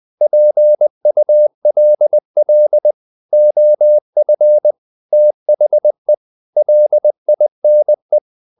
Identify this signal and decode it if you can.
Morse code, heard as a keyed tone, about 20 wpm, 600 Hz. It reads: PULL OF THE LINE
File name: the old man and the sea part 2 20wpm